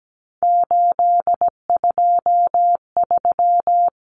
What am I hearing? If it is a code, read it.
Morse code, heard as a keyed tone, about 17 wpm, 700 Hz. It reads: 823
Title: Common Cut Numbers - 3-Digit 17wpm